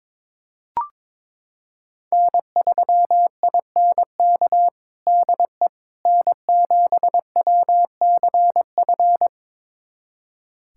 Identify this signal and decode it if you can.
Morse code, heard as a keyed tone, about 22 wpm, 700 Hz. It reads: N3INK DE N7WCF
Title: USA Callsign DE Callsign - 5 Characters - 22wpm